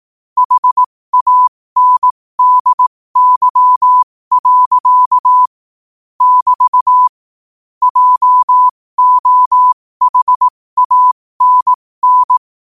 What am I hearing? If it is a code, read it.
Morse code, heard as a keyed tone, about 18 wpm, spaced as for 15 wpm, 1000 Hz. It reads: HANDY. <BT> JOHANN
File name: qotdfiles / 15wpm